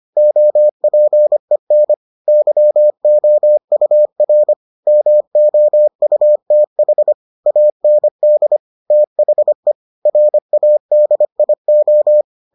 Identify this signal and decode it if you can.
Morse code, heard as a keyed tone, about 25 wpm, 600 Hz. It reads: OPEN YOUR MOUTH AND THE RADIO